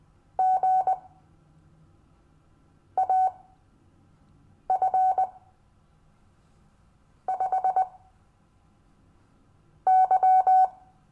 Morse code, heard as a keyed tone, about 20 wpm, spaced as for 4 wpm, 750 Hz. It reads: GAF5Y
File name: cw / Lesson15 five forwardslash